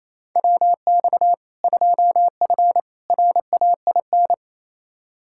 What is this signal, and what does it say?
Morse code, heard as a keyed tone, about 27 wpm, 700 Hz. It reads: WX 2F RAIN
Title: K1USN SST QSO Practice 28wpm